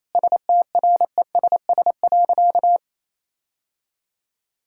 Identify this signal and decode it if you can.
Morse code, heard as a keyed tone, about 28 wpm, 700 Hz. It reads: STRESS.
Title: Sentences from Top 1000 Words - Review Part 6 - 28wpm